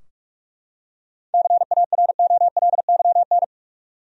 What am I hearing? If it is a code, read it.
Morse code, heard as a keyed tone, about 43 wpm, 700 Hz. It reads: CAROLYN
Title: USA Common Names - 45wpm